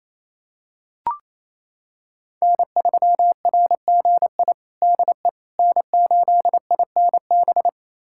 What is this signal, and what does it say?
Morse code, heard as a keyed tone, about 28 wpm, 700 Hz. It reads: N3RGI DE N8INB
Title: USA Callsign DE Callsign - 5 Characters - 28wpm